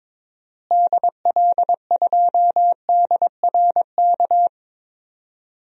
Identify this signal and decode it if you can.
Morse code, heard as a keyed tone, about 22 wpm, 700 Hz. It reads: DL2DRK
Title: DXCC-100 Call Signs - 6 Characters - Repeated 2x - 22wpm